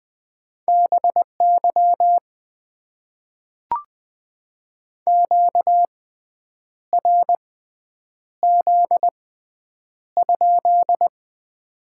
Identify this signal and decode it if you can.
Morse code, heard as a keyed tone, about 20 wpm, spaced as for 8 wpm, 700 Hz. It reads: BY QRZ?
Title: Top 90 QSO Elements - 6x Character Spacing - 20wpm